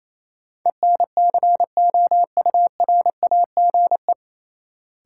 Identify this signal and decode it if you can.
Morse code, heard as a keyed tone, about 28 wpm, 700 Hz. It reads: ENCOURAGE
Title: Sentences from Top 1400 Words - New Word Drill - 28wpm